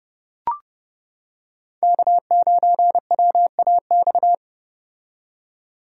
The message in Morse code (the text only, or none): K9WAX